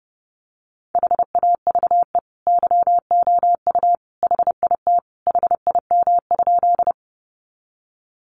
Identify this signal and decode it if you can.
Morse code, heard as a keyed tone, about 28 wpm, 700 Hz. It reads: HAVE YOU HIT HIM?
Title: Sentences from Top 1200 Words 30wpm